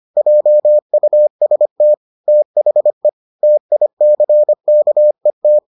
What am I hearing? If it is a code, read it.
Morse code, heard as a keyed tone, about 25 wpm, 600 Hz. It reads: JUST THE TICKET